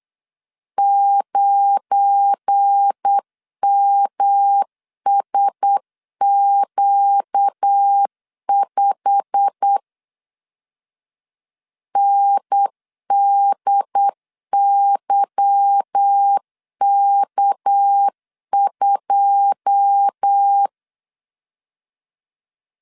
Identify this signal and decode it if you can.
Morse code, heard as a keyed tone, about 8 wpm, 800 Hz. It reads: 9MSQ5 NDYK2
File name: LEZIONI / 9 2 8wpm 3437